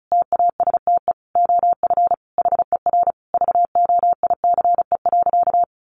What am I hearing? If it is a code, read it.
Morse code, heard as a keyed tone, about 34 wpm, 700 Hz. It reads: TASTE OF HER VOICE.
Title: Sentences from Top 1100 Words 35wpm